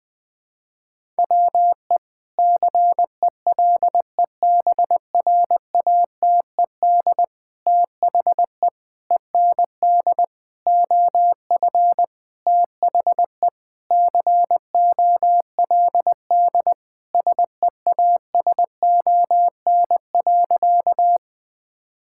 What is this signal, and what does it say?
Morse code, heard as a keyed tone, about 19 wpm, 700 Hz. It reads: WE CELEBRATED THE END OF THE COLD SEASON.
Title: Sentences with 5-Word Phrases - 20wpm